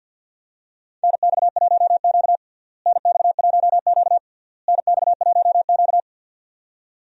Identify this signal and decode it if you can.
Morse code, heard as a keyed tone, about 50 wpm, 700 Hz. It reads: NX1Q NX1Q NX1Q
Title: USA Call Signs - 4 Characters - Repeated 3x - 50wpm